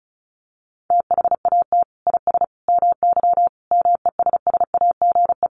Morse code, heard as a keyed tone, about 34 wpm, 700 Hz. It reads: THAT IS MY MESSAGE
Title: Sentences from Top 1100 Words 35wpm